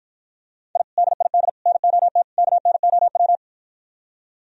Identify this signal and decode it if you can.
Morse code, heard as a keyed tone, about 53 wpm, 700 Hz. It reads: I DID NOT KNOW